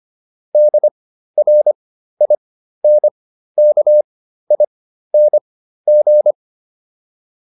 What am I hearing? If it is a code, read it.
Morse code, heard as a keyed tone, about 25 wpm, spaced as for 13 wpm, 600 Hz. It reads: DRINKING
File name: ING8 13